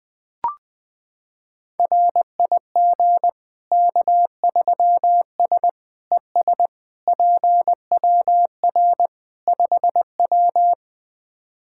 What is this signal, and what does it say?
Morse code, heard as a keyed tone, about 21 wpm, 700 Hz. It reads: RIG K3S ES PWR 5W
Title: K1USN SST QSO Practice 20wpm